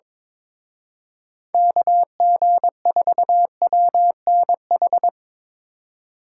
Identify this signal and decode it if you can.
Morse code, heard as a keyed tone, about 24 wpm, 700 Hz. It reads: KG4WNH
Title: Speed Racing - USA 2x3 Call Signs 22wpm